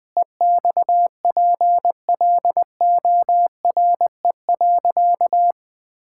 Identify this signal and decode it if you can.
Morse code, heard as a keyed tone, about 21 wpm, 700 Hz. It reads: EXPLORE.
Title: Sentences with 2-Word Phrases - 20wpm